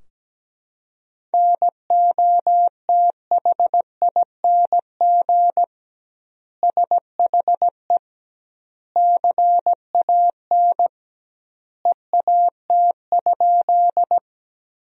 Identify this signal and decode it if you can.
Morse code, heard as a keyed tone, about 17 wpm, 700 Hz. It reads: NOTHING SHE CAN EAT?
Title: 2x Word-Spacing - Sentences from Top 500 Words - 17wpm